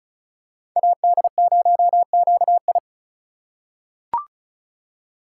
Morse code, heard as a keyed tone, about 36 wpm, 700 Hz. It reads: AD0QI
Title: USA Call Signs - 5 Characters - 2x Character Spacing - 35wpm